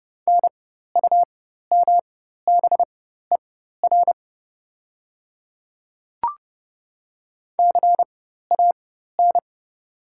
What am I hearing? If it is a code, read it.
Morse code, heard as a keyed tone, about 30 wpm, spaced as for 14 wpm, 700 Hz. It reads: NUMBER CAN
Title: 4x Character Spacing - Top 200 Words - 30wpm